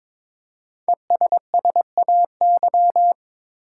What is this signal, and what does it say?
Morse code, heard as a keyed tone, about 23 wpm, 700 Hz. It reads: ESSAY
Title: Five Letter Words - 22wpm